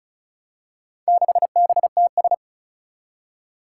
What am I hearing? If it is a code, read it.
Morse code, heard as a keyed tone, about 36 wpm, 700 Hz. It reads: 6BTS
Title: Four Challenging Letter-Number Combinations 35wpm